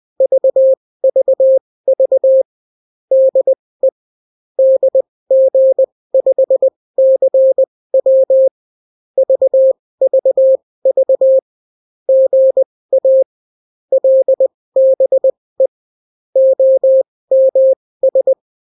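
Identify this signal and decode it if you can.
Morse code, heard as a keyed tone, about 20 wpm, spaced as for 16 wpm, 530 Hz. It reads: VVV DE DG5CW VVV GA LBE OMS
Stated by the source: Daily Morse weather and top stories at 16 wpm, clean without QRN.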